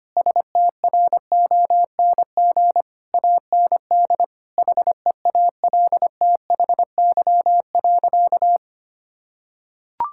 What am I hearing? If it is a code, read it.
Morse code, heard as a keyed tone, about 25 wpm, 700 Hz. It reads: STRONG AND HEALTHY.
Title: Sentences from Top 1000 Words - Review Part 5 - 25wpm